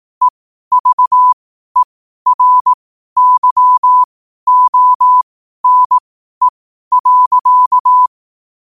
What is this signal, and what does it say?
Morse code, heard as a keyed tone, about 18 wpm, spaced as for 12 wpm, 1000 Hz. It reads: EVERYONE.
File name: qotdfiles / 12wpm